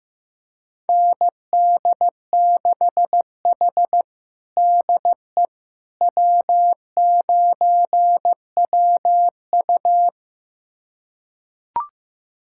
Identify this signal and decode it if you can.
Morse code, heard as a keyed tone, about 15 wpm, 700 Hz. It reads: ND6H DE W9WU
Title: USA Callsign DE Callsign - 4 Characters - 15wpm